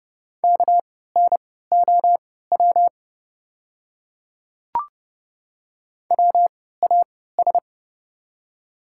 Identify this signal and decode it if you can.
Morse code, heard as a keyed tone, about 30 wpm, spaced as for 17 wpm, 700 Hz. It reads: KNOW WAS
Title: 3x Character Spacing - Top 200 Words - 30wpm